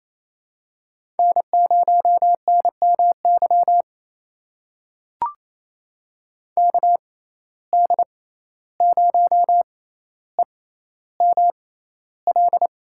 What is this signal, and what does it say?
Morse code, heard as a keyed tone, about 28 wpm, spaced as for 14 wpm, 700 Hz. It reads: N0NMY KD0EML
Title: USA Call Signs - 6x Character Spacing - 28wpm